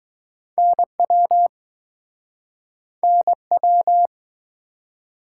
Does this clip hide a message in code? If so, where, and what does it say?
Yes: Morse code from 0.6 to 4.1 s, heard as a keyed tone, about 22 wpm, 700 Hz. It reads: NW NW